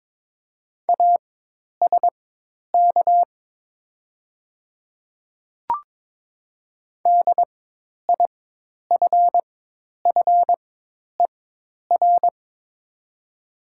Morse code, heard as a keyed tone, about 22 wpm, spaced as for 10 wpm, 700 Hz. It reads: ASK DIFFER
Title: 4x Character Spacing - Top 200 Words - 22wpm